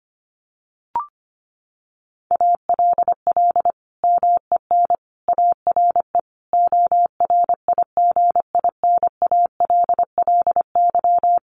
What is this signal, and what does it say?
Morse code, heard as a keyed tone, about 25 wpm, 700 Hz. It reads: ALL MEN ARE ORIGINALLY
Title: Sentences from Top 1500 Words 25wpm